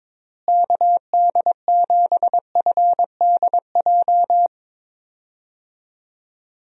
Morse code, heard as a keyed tone, about 22 wpm, 700 Hz. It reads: KD7FDJ
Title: USA Call Signs - 6 Characters - Repeated 3x - 22wpm